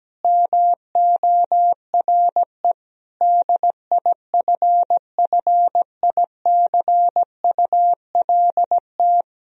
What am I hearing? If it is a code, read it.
Morse code, heard as a keyed tone, about 17 wpm, 700 Hz. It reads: MORE DIFFICULT
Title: Sentences from Top 1000 Words - Review Part 3 - 17wpm